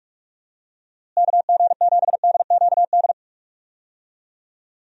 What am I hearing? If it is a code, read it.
Morse code, heard as a keyed tone, about 45 wpm, 700 Hz. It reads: KG7DQD
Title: USA Call Signs - 6 Characters - Repeated 2x - 45wpm